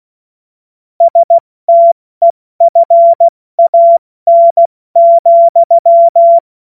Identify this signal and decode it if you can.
Morse code, heard as a keyed tone, about 16 wpm, spaced as for 14 wpm, 680 Hz. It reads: STEFAN,
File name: Morsekurs / Witze / Witz 30 Januar 2024 70 F80